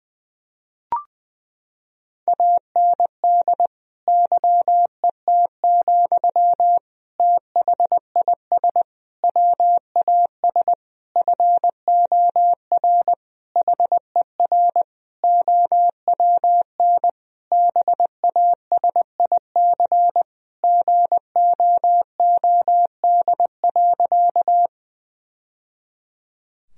Morse code, heard as a keyed tone, about 21 wpm, 700 Hz. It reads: AND YET, THIS WAS FOR HER OWN BASIC GOOD.
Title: Sentences from Top 1000 Words - Review Part 3 - 20wpm